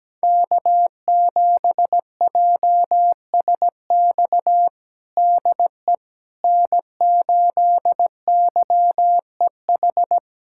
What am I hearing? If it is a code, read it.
Morse code, heard as a keyed tone, about 17 wpm, 700 Hz. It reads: K7JSX DE N8YEH
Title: USA Callsign DE Callsign - 5 Characters - 17wpm